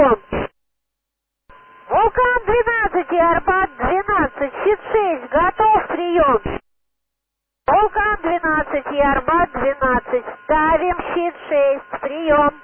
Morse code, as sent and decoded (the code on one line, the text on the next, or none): none